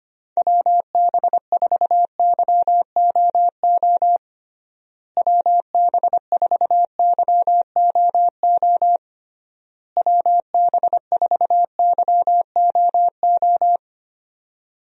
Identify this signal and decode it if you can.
Morse code, heard as a keyed tone, about 25 wpm, 700 Hz. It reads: WB4YOO WB4YOO WB4YOO
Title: USA Call Signs - 6 Characters - Repeated 3x - 25wpm